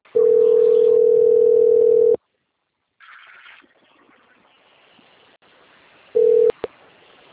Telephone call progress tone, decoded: ringback tone

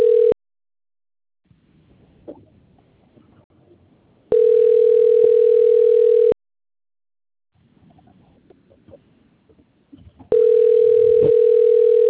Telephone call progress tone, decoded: ringback tone

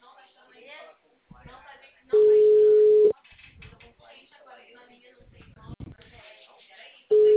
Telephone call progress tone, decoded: ringback tone

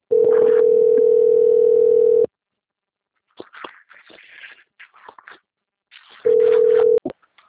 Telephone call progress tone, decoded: ringback tone

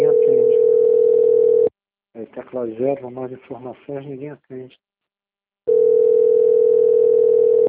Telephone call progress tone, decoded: ringback tone

